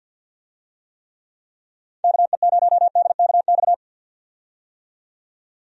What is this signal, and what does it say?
Morse code, heard as a keyed tone, about 45 wpm, 700 Hz. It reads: KE0DKX
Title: USA Call Signs - 6 Characters - 50wpm